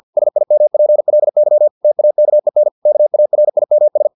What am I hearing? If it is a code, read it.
Morse code, heard as a keyed tone, about 50 wpm, 600 Hz. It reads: SIMPLY TAKEN KARIMS
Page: Never 49wpm